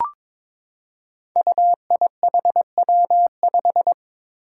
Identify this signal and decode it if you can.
Morse code, heard as a keyed tone, about 22 wpm, 700 Hz. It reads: UIHW5
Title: Five Challenging Letter-Number Combinations 22wpm